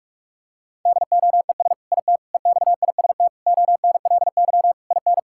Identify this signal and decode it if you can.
Morse code, heard as a keyed tone, about 47 wpm, 700 Hz. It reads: DOES IT EXIST ONLY IN
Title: Sentences from Top 1000 Words - Review Part 5 - 45wpm